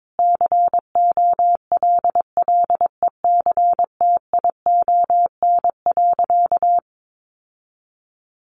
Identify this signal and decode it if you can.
Morse code, heard as a keyed tone, about 22 wpm, 700 Hz. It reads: COLLECTION.
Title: Sentences from Top 1200 Words 22wpm